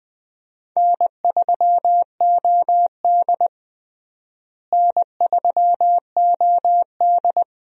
Morse code, heard as a keyed tone, about 20 wpm, 700 Hz. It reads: N3OD N3OD
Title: USA Call Signs - 4 Characters - Repeated 2x - 20wpm